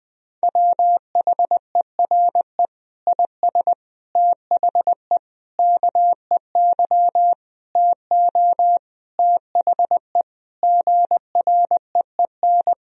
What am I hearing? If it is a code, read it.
Morse code, heard as a keyed tone, about 20 wpm, 700 Hz. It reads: WHERE IS THE KEY TO THE GREEN
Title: Sentences from Top 1000 Words - Review Part 4 - 20wpm